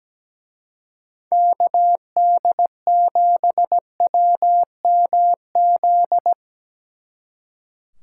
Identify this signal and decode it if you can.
Morse code, heard as a keyed tone, about 19 wpm, 700 Hz. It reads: KD7WMZ
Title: Speed Racing - USA 2x3 Call Signs 17wpm